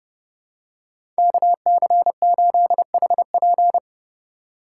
Morse code, heard as a keyed tone, about 32 wpm, 700 Hz. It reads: KC8HP
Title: USA Call Signs - 5 Characters - 2x Character Spacing - 30wpm